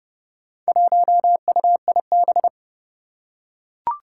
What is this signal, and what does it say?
Morse code, heard as a keyed tone, about 30 wpm, 690 Hz. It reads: 1UIB
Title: Four Challenging Letter-Number Combinations 30wpm